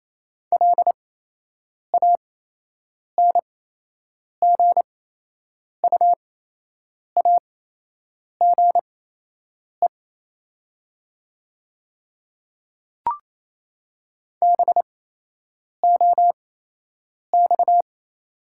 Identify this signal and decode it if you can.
Morse code, heard as a keyed tone, about 28 wpm, spaced as for 8 wpm, 700 Hz. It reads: LANGUAGE BOX
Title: 8x Character Spacing - Top 500 Words - 28wpm